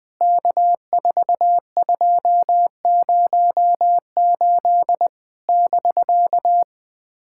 Morse code, heard as a keyed tone, about 20 wpm, 700 Hz. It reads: K4208 <BK>
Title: POTA QSO Practice - 20wpm